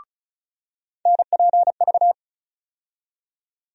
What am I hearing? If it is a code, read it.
Morse code, heard as a keyed tone, about 36 wpm, 700 Hz. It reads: NPV
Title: USA Call Sign Postfixes - 3 Letters 35wpm